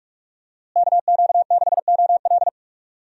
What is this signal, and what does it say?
Morse code, heard as a keyed tone, about 45 wpm, 700 Hz. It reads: KQ6OL
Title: USA Call Signs - 5 Characters - Repeated 3x - 45wpm